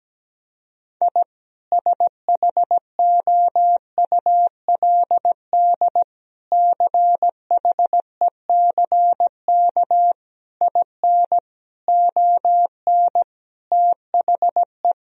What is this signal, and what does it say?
Morse code, heard as a keyed tone, about 17 wpm, 700 Hz. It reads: I SHOULD CHECK IN ON THE
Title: Sentences from Top 1000 Words - Review Part 3 - 17wpm